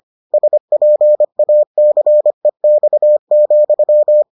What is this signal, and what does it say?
Morse code, heard as a keyed tone, about 25 wpm, 600 Hz. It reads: SPACEX,